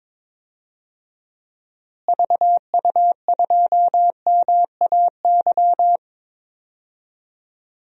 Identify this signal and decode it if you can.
Morse code, heard as a keyed tone, about 22 wpm, 700 Hz. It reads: VU2MAY
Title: DXCC-100 Call Signs - 6 Characters - Repeated 2x - 22wpm